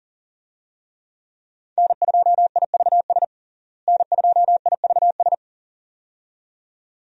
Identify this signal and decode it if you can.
Morse code, heard as a keyed tone, about 40 wpm, 700 Hz. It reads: N2IVS N2IVS
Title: USA Call Signs - 5 Characters - Repeated 2x - 40wpm